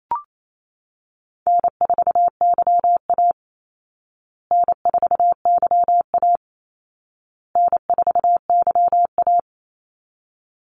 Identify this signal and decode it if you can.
Morse code, heard as a keyed tone, about 28 wpm, 700 Hz. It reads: N4YA N4YA N4YA
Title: Easy Call Signs Repeated 3x - 28wpm